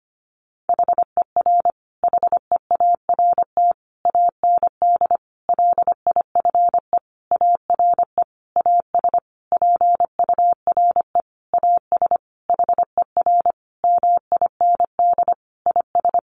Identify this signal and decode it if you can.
Morse code, heard as a keyed tone, about 25 wpm, 700 Hz. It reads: HER HEART AND LIFE ARE AS PURE AS HER MIND IS
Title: Sentences from Top 1500 Words 25wpm